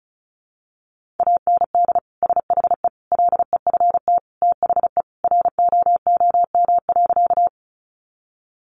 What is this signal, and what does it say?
Morse code, heard as a keyed tone, about 34 wpm, 700 Hz. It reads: AND SHE LEFT THE ROOM.
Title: Sentences from Top 300 Words 35wpm